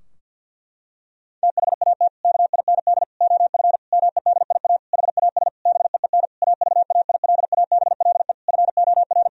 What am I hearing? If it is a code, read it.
Morse code, heard as a keyed tone, about 50 wpm, 700 Hz. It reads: THAT KIND OF MEDIA HAS BEEN AVAILABLE FOR
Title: Sentences from Top 1000 Words - Review Part 4 - 50wpm